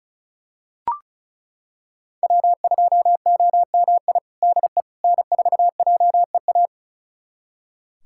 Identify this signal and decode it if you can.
Morse code, heard as a keyed tone, about 36 wpm, 700 Hz. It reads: W2OMI DE N4JEA
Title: USA Callsign DE Callsign - 5 Characters - 35wpm